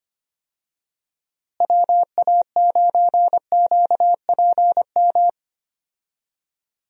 Morse code, heard as a keyed tone, about 25 wpm, 700 Hz. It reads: WA9QPM
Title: USA Call Signs - 6 Characters - Repeated 3x - 25wpm